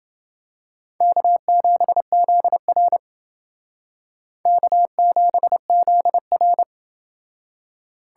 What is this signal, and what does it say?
Morse code, heard as a keyed tone, about 29 wpm, 700 Hz. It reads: K7ZR K7ZR